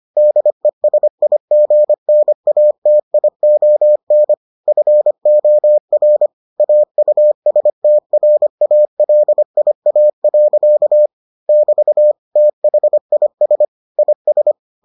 Morse code, heard as a keyed tone, about 25 wpm, 600 Hz. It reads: DESIGNATION FOR AUSTRALIA. <BT> THIS IS